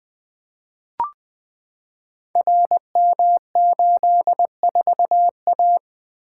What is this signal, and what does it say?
Morse code, heard as a keyed tone, about 20 wpm, 700 Hz. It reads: RM84A
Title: Code Groups - 5 Letter:Numbers - 20wpm